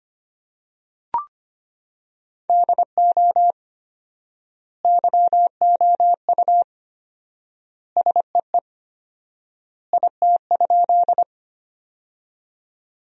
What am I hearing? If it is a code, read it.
Morse code, heard as a keyed tone, about 25 wpm, 700 Hz. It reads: DO YOU SEE IT?
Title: 4x Word-Spacing - Sentences from Top 100 Words - 25wpm